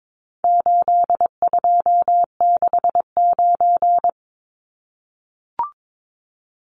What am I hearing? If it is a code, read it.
Morse code, heard as a keyed tone, about 22 wpm, 700 Hz. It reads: 8269
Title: Numbers - 4-Digit 22wpm